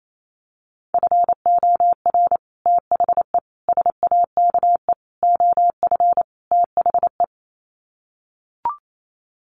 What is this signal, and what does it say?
Morse code, heard as a keyed tone, about 28 wpm, 700 Hz. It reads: FOR THE SAKE OF THE
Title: Sets of 5 Words Encore 28wpm